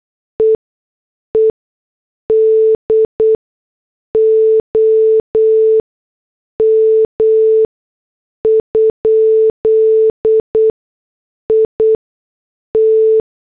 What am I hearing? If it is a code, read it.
Morse code, heard as a keyed tone, about 8 wpm, spaced as for 6 wpm, 440 Hz. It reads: EEDOM? IT